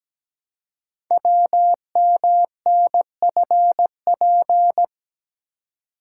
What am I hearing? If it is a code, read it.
Morse code, heard as a keyed tone, about 17 wpm, 700 Hz. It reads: WMNFP
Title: Code Groups - 5 Letters - 17wpm